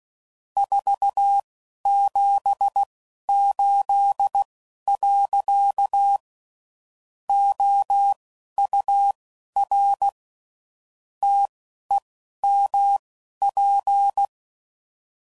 Morse code, heard as a keyed tone, about 16 wpm, spaced as for 11 wpm, 800 Hz. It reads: 478. OUR TEMP
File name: CW Code Groups @10WPM / QSOs / QSO WD0FBT